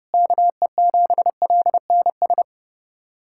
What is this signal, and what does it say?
Morse code, heard as a keyed tone, about 29 wpm, 700 Hz. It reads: KE7LNS
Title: USA Callsign DE Callsign - 6 Characters - 30wpm